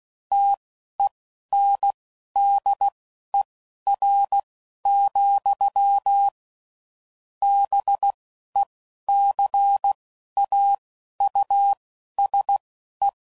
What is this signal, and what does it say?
Morse code, heard as a keyed tone, about 16 wpm, spaced as for 11 wpm, 800 Hz. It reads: TENDER, BECAUSE